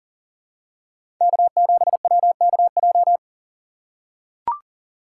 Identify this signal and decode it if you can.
Morse code, heard as a keyed tone, about 40 wpm, 700 Hz. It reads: K7WKJ
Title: USA Call Signs - 4x Character Spacing - 40wpm